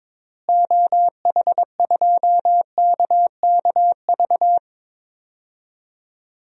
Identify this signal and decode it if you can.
Morse code, heard as a keyed tone, about 22 wpm, 700 Hz. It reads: OH2KKV
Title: DXCC-100 Call Signs - 6 Characters - Repeated 3x - 22wpm